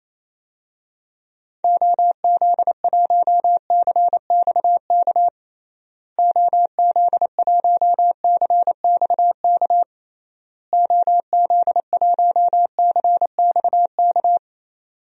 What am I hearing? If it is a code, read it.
Morse code, heard as a keyed tone, about 28 wpm, 700 Hz. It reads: OZ1CXK OZ1CXK OZ1CXK
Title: DXCC-100 Call Signs - 6 Characters - Repeated 3x - 28wpm